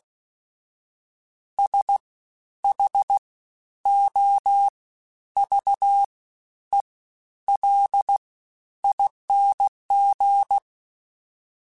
Morse code, heard as a keyed tone, about 16 wpm, spaced as for 10 wpm, 800 Hz. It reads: SHOVELING
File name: Special INGs-mp3s / Special INGs-mp3s / SlowToFaster ing @15